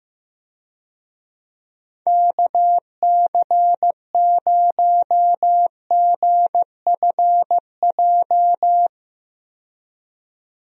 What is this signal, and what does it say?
Morse code, heard as a keyed tone, about 15 wpm, 700 Hz. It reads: KC0GFJ
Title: USA Call Signs - 6 Characters - 15wpm